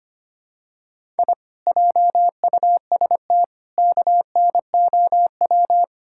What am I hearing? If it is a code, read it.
Morse code, heard as a keyed tone, about 25 wpm, 700 Hz. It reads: I JUST KNOW
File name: practice / 25wpm / Sentences with 2-Word Phrases - 25wpm